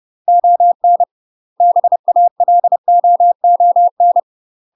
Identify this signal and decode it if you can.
Morse code, heard as a keyed tone, about 30 wpm, 700 Hz. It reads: ON BALOON
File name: UK MASTER QSOs TRAINING / 30WPM / UK MASTER QSOs 0021